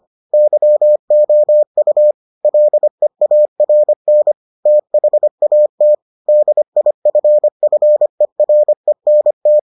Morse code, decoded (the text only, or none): YOU LEARN THAT DIFFERENT